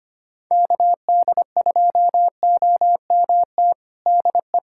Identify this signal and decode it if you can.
Morse code, heard as a keyed tone, about 25 wpm, 700 Hz. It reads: KD2OMT DE
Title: USA Callsign DE Callsign 25wpm